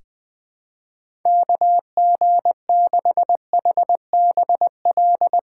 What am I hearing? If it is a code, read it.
Morse code, heard as a keyed tone, about 20 wpm, 700 Hz. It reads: KG6HBL
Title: USA Call Signs - 2x Character Spacing - 20wpm